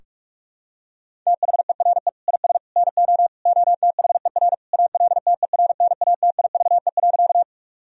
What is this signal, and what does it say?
Morse code, heard as a keyed tone, about 46 wpm, 700 Hz. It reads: THERE IS NO OTHER ALTERNATIVE.
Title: Sentences from Top 1000 Words - Review Part 4 - 45wpm